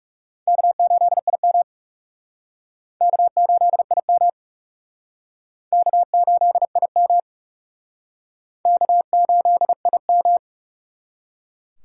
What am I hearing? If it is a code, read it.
Morse code, heard as a keyed tone, about 38 wpm, 700 Hz. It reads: K8IM K8IM K8IM K8IM
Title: Speed Racing - USA 1x2 Call Signs 30wpm